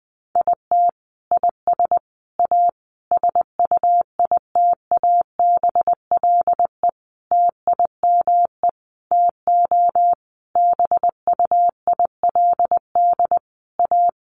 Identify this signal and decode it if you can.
Morse code, heard as a keyed tone, about 20 wpm, 700 Hz. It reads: IT IS A SUITABLE TIME TO BUILD A
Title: Sentences from Top 1400 Words 20wpm